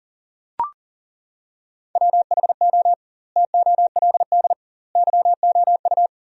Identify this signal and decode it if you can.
Morse code, heard as a keyed tone, about 40 wpm, 700 Hz. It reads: WHO TOLD YOU
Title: 2x Word-Spacing - Sentences from Top 400 Words - 40wpm